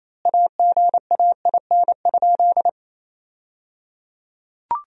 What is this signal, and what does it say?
Morse code, heard as a keyed tone, about 28 wpm, 700 Hz. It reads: AGAIN?
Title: 3x Word-Spacing - Sentences from Top 200 Words - 28wpm